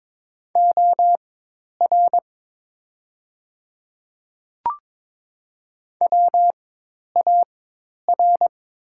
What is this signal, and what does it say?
Morse code, heard as a keyed tone, about 22 wpm, spaced as for 10 wpm, 700 Hz. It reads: OR WAR
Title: 4x Character Spacing - Top 500 Words - 22wpm